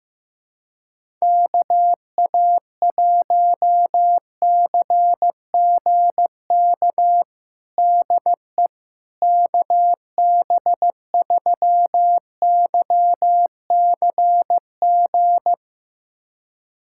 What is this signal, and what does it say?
Morse code, heard as a keyed tone, about 15 wpm, 700 Hz. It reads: KA1CGK DE KB3YCG
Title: USA Callsign DE Callsign - 6 Characters - 15wpm